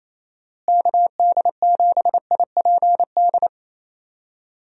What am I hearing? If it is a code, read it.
Morse code, heard as a keyed tone, about 27 wpm, 700 Hz. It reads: KD7IPD
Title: USA Call Signs - 6 Characters - Repeated 2x - 28wpm